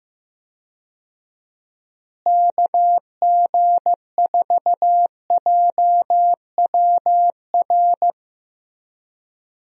Morse code, heard as a keyed tone, about 15 wpm, 700 Hz. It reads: KG4JWR
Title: USA Call Signs - 6 Characters - Repeated 2x - 15wpm